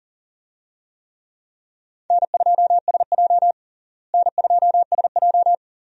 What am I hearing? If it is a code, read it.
Morse code, heard as a keyed tone, about 40 wpm, 700 Hz. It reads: N2SJ N2SJ
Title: USA Call Signs - 4 Characters - Repeated 2x - 40wpm